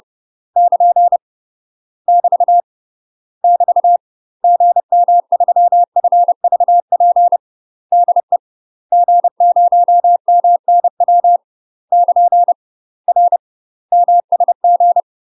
Morse code, heard as a keyed tone, about 30 wpm, 700 Hz. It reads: <KN> <BT> <BT> GM3FVP DE G0MNW <KN> R MSG
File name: UK MASTER QSOs TRAINING / 30WPM / UK MASTER QSOs 0041